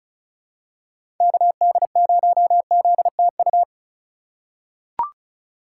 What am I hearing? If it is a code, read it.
Morse code, heard as a keyed tone, about 36 wpm, 700 Hz. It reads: KD0ZTU
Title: USA Call Signs - 6 Characters - 4x Character Spacing - 35wpm